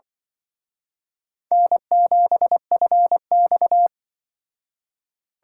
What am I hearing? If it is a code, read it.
Morse code, heard as a keyed tone, about 24 wpm, 700 Hz. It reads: N7FX
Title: Speed Racing - USA Call Signs 20wpm